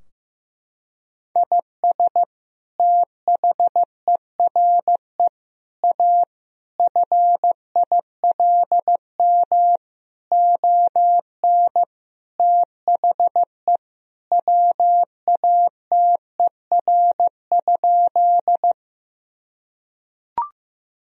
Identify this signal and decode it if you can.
Morse code, heard as a keyed tone, about 15 wpm, 700 Hz. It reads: IS THERE A FILM ON THE WATER?
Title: Sentences from Top 1000 Words - Review Part 3 - 15wpm